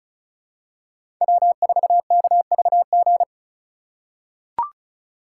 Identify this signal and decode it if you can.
Morse code, heard as a keyed tone, about 36 wpm, 700 Hz. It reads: W4KVG
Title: USA Call Signs - 5 Characters - 4x Character Spacing - 35wpm